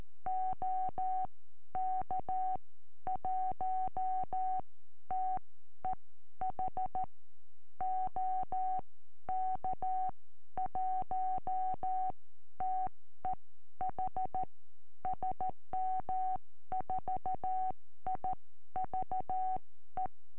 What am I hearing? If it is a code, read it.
Morse code, heard as a keyed tone, about 13 wpm, spaced as for 11 wpm, 750 Hz. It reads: OK1TEH OK1TEH SM4IVE
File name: eme / sm4ive 70cm